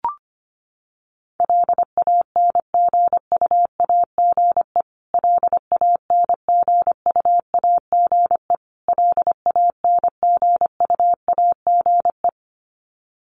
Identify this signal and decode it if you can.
Morse code, heard as a keyed tone, about 25 wpm, 700 Hz. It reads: LANGUAGE LANGUAGE LANGUAGE
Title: Top 500 Words Repeated 25wpm